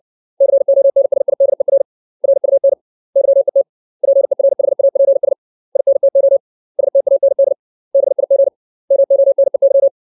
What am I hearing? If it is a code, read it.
Morse code, heard as a keyed tone, about 60 wpm, 550 Hz. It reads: CONSIDER RUN YET PERHAPS INTO STAND BIG MONEY
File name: 632 words / 632 Words 60WPM